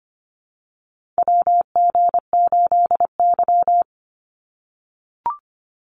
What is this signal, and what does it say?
Morse code, heard as a keyed tone, about 25 wpm, 700 Hz. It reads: WG8Y
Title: NA SOTA Chaser Callsigns 25wpm